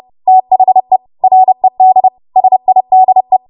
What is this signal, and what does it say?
Morse code, heard as a keyed tone, about 30 wpm, 750 Hz. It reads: THE RED SIDE
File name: Morse / 30WPM